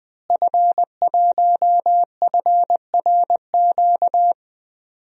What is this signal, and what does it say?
Morse code, heard as a keyed tone, about 20 wpm, 700 Hz. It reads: F1FRQ
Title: DXCC-100 Call Signs - 5 Characters - Repeated 3x - 20wpm